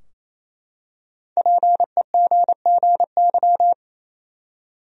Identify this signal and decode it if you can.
Morse code, heard as a keyed tone, about 28 wpm, 700 Hz. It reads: PEGGY
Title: USA Common Names - 28wpm